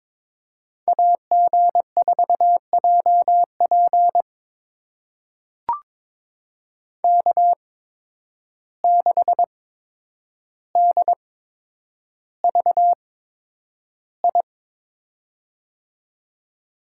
Morse code, heard as a keyed tone, about 23 wpm, spaced as for 9 wpm, 700 Hz. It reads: AG4JP K6DVI
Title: USA Call Signs - 5 Characters - 8x Character Spacing - 22wpm